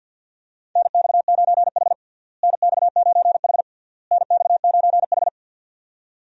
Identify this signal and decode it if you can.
Morse code, heard as a keyed tone, about 51 wpm, 700 Hz. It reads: NX9H NX9H NX9H
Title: USA Call Signs - 4 Characters - Repeated 3x - 50wpm